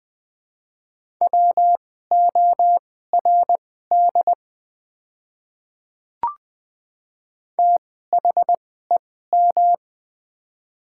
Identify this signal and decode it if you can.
Morse code, heard as a keyed tone, about 21 wpm, spaced as for 15 wpm, 700 Hz. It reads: WORD THEM
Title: 2x Character Spacing - Top 100 Words - 20wpm